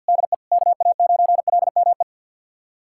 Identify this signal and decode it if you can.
Morse code, heard as a keyed tone, about 52 wpm, 700 Hz. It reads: DE KA9LME
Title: USA Callsign DE Callsign 50wpm